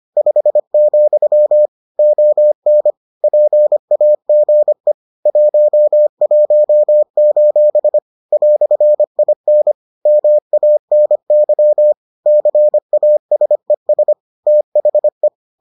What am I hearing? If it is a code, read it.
Morse code, heard as a keyed tone, about 25 wpm, 600 Hz. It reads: 5, ON PAGE 11: "IN MANY CASES THE